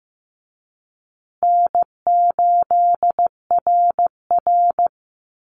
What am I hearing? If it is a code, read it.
Morse code, heard as a keyed tone, about 15 wpm, 700 Hz. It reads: N8RR
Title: Easy Call Signs Repeated 3x - 15wpm